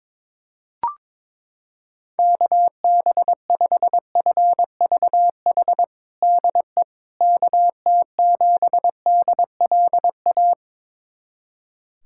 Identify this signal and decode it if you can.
Morse code, heard as a keyed tone, about 22 wpm, 700 Hz. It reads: KB5FVH DE KT7DLA
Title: USA Callsign DE Callsign 22wpm